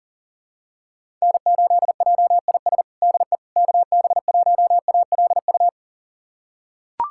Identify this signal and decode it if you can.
Morse code, heard as a keyed tone, about 37 wpm, 700 Hz. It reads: N8JIS DE KB1ALU
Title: USA Callsign DE Callsign 40wpm